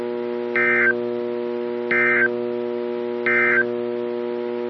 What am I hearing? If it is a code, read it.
Emergency Alert System (SAME) tones, heard as harsh buzzing bursts. It is heard from 0.5 to 3.7 s.
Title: NOAA Weather Radio - KEC55